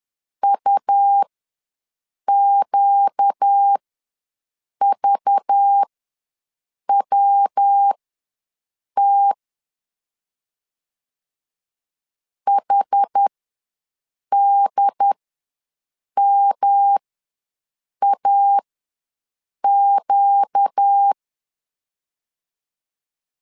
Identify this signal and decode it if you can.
Morse code, heard as a keyed tone, about 10 wpm, spaced as for 6 wpm, 800 Hz. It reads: UQVWT HDMAQ
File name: LEZIONI / 5 LBVC / 5 LBV ws 10wpm 3645